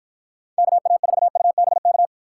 Morse code, heard as a keyed tone, about 48 wpm, 700 Hz. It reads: KA4UBK